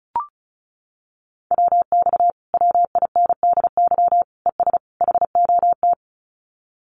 Morse code, heard as a keyed tone, about 34 wpm, 700 Hz. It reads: WX WINDY ES HOT
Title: Common QSO Phrases Encore 35wpm